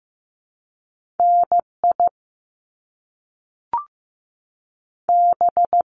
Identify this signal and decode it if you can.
Morse code, heard as a keyed tone, about 15 wpm, 700 Hz. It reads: N I B
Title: Top Two Letter Combinations in Words 15wpm